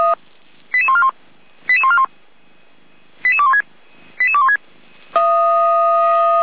Five-tone selective call (ZVEI), heard as quick runs of five tones, tones of about 70 ms. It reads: 88141 88141 88216 88216